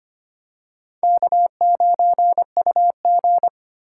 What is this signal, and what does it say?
Morse code, heard as a keyed tone, about 25 wpm, 700 Hz. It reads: K9UG
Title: USA Call Signs - 4 Characters - 2x Character Spacing - 25wpm